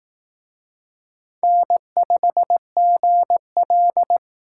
Morse code, heard as a keyed tone, about 18 wpm, 700 Hz. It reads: N5GL
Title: Speed Racing - USA Call Signs 15wpm